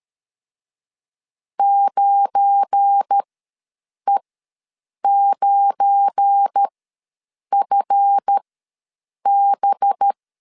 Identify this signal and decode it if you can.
Morse code, heard as a keyed tone, about 13 wpm, spaced as for 7 wpm, 800 Hz. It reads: 9E9FB
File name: LEZIONI / 7 GF9Z / 7 GF9 ws 12wpm 3344